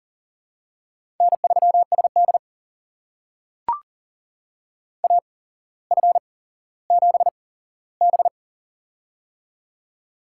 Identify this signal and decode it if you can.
Morse code, heard as a keyed tone, about 42 wpm, spaced as for 17 wpm, 700 Hz. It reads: N3SD AF7B
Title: USA Call Signs - 4 Characters - 8x Character Spacing - 40wpm